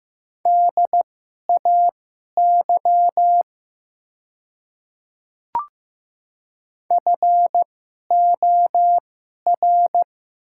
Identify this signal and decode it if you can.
Morse code, heard as a keyed tone, about 15 wpm, spaced as for 11 wpm, 700 Hz. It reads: DAY FOR
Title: 2x Character Spacing - Top 100 Words - 15wpm